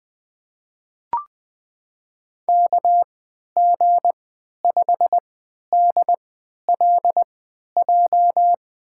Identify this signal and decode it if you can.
Morse code, heard as a keyed tone, about 21 wpm, spaced as for 11 wpm, 700 Hz. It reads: KG5DLJ
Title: USA Call Signs - 6 Characters - 3x Character Spacing - 20wpm